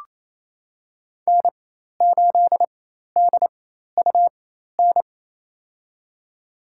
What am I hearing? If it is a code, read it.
Morse code, heard as a keyed tone, about 28 wpm, spaced as for 13 wpm, 700 Hz. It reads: N8DUN
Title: USA Call Signs - 4x Character Spacing - 28wpm